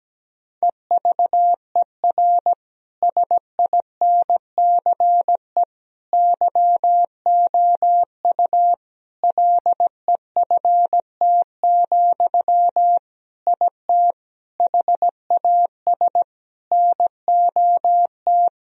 Morse code, heard as a keyed tone, about 17 wpm, 700 Hz. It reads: EVER SINCE YOU LEFT, IT HAS NOT
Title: Sentences from Top 1000 Words - Review Part 1 - 17wpm